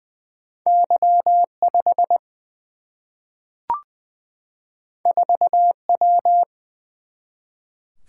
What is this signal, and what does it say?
Morse code, heard as a keyed tone, about 20 wpm, 700 Hz. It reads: Y 5 4 W
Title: Single-Letter Single-Number 20wpm